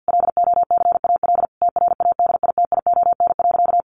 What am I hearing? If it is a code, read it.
Morse code, heard as a keyed tone, about 50 wpm, 700 Hz. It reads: LOCAL TRADITION.
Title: Sentences from Top 1500 Words 50wpm